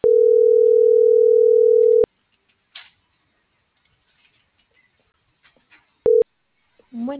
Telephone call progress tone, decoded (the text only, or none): ringback tone